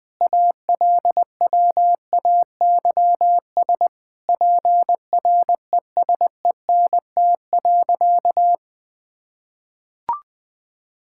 Morse code, heard as a keyed tone, about 19 wpm, 700 Hz. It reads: ALWAYS PRESENT.
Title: Sentences from Top 1000 Words - Review Part 5 - 20wpm